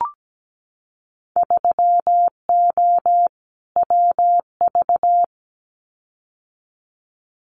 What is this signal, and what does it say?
Morse code, heard as a keyed tone, about 17 wpm, 700 Hz. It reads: 3O WV
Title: Winter Field Day Exchanges 17wpm